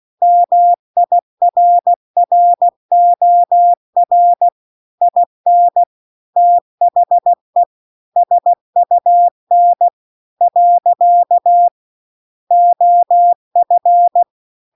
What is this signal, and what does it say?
Morse code, heard as a keyed tone, about 16 wpm, 700 Hz. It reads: MIRROR IN THE SUN . OF